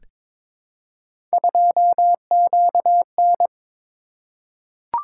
Morse code, heard as a keyed tone, about 23 wpm, 700 Hz. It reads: 2QN
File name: practice / 22wpm / Three Letter-Number Combinations - 22wpm